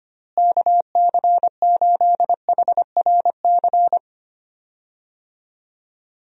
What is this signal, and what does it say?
Morse code, heard as a keyed tone, about 25 wpm, 700 Hz. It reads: KC8HRC
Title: USA Call Signs - 6 Characters - Repeated 3x - 25wpm